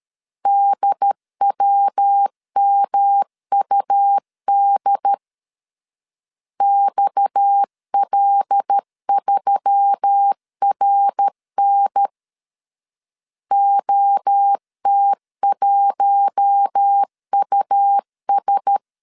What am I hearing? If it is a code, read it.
Morse code, heard as a keyed tone, about 13 wpm, 800 Hz. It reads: DWMUD XL3RN OT1US
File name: CW / DET / 1 6 1500C S A 12wpm 3440